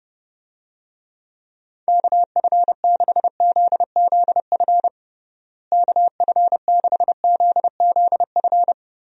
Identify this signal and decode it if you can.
Morse code, heard as a keyed tone, about 30 wpm, 700 Hz. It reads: KF6ZZF KF6ZZF
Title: USA Call Signs - 6 Characters - Repeated 3x - 30wpm